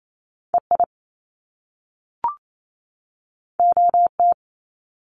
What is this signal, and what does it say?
Morse code, heard as a keyed tone, about 28 wpm, 700 Hz. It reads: EI OT
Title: Top Two Letter Combinations in Words 28wpm